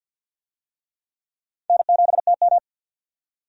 Morse code, heard as a keyed tone, about 53 wpm, 700 Hz. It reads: N7TM
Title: USA Call Signs - 4 Characters - 50wpm